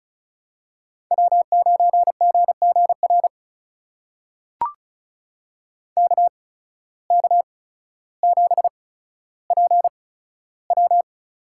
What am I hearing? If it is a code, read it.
Morse code, heard as a keyed tone, about 36 wpm, spaced as for 15 wpm, 700 Hz. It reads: W9GGR KK7PW
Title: USA Call Signs - 5 Characters - 8x Character Spacing - 35wpm